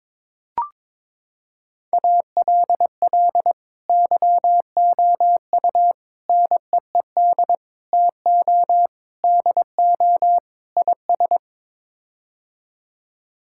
Morse code, heard as a keyed tone, about 22 wpm, 700 Hz. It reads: ALL YOU NEED TO DO IS
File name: practice / 22wpm / Sets of 6 Words - 22wpm